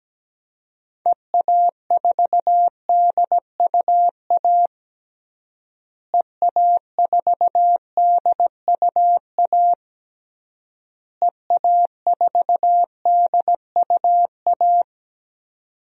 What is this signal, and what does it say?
Morse code, heard as a keyed tone, about 17 wpm, 700 Hz. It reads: EA4DUA EA4DUA EA4DUA
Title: DXCC-100 Call Signs - 6 Characters - Repeated 3x - 17wpm